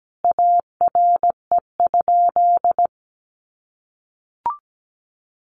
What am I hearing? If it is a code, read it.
Morse code, heard as a keyed tone, about 17 wpm, 700 Hz. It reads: ARE?
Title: Sentences from Top 1400 Words 17wpm